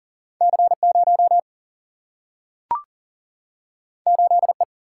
Morse code, heard as a keyed tone, about 40 wpm, 700 Hz. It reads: C0 8E
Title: Single-Letter Single-Number 40wpm